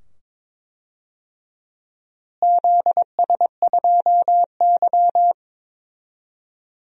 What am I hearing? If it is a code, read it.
Morse code, heard as a keyed tone, about 22 wpm, 700 Hz. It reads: ZS2Y
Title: DXCC-100 Call Signs - 4 Characters - Repeated 2x - 22wpm